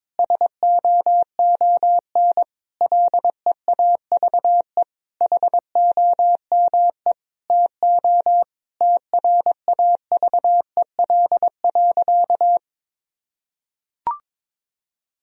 Morse code, heard as a keyed tone, about 22 wpm, 700 Hz. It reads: SOON LEAVE HOME TO TRAVEL.
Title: Sentences from Top 1000 Words - Review Part 2 - 22wpm